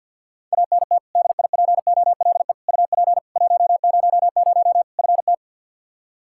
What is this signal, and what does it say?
Morse code, heard as a keyed tone, about 52 wpm, 700 Hz. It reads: ANT DIPOLE UP 100 FT
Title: Practice QSO - Exchange 2 - Info - 50wpm